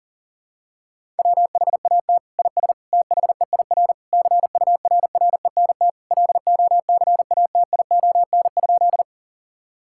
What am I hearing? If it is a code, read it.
Morse code, heard as a keyed tone, about 40 wpm, 700 Hz. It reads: WHAT IS THEIR CURRENT LOCATION?
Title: Sentences from Top 1000 Words - Review Part 6 - 40wpm